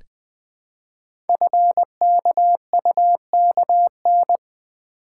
Morse code, heard as a keyed tone, about 20 wpm, 700 Hz. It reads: FKUKN
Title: Code Groups - 5 Letters - 20wpm